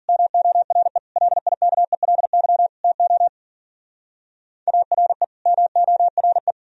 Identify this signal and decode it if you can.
Morse code, heard as a keyed tone, about 49 wpm, 700 Hz. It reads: MORE LIKELY TO ARE MORE
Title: Speed-Racing - Sets of 4 Words 40wpm